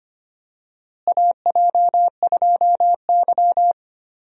Morse code, heard as a keyed tone, about 25 wpm, 700 Hz. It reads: AJ2Y
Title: USA Call Signs - 4 Characters - Repeated 3x - 25wpm